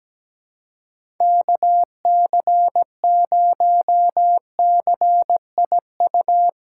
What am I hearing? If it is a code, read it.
Morse code, heard as a keyed tone, about 17 wpm, 700 Hz. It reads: KC0CIU
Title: USA Callsign DE Callsign - 6 Characters - 17wpm